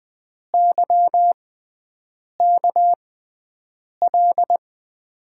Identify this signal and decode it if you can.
Morse code, heard as a keyed tone, about 21 wpm, spaced as for 7 wpm, 700 Hz. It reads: YKL
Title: USA Call Signs - 5 Characters - 6x Character Spacing - 20wpm